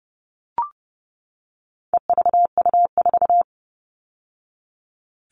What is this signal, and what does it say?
Morse code, heard as a keyed tone, about 30 wpm, 700 Hz. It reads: EVU4
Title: All Cut Numbers - 4-Digit 30wpm